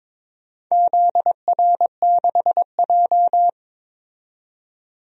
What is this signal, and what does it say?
Morse code, heard as a keyed tone, about 22 wpm, 700 Hz. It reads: ZR6J
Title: DXCC-100 Call Signs - 4 Characters - 22wpm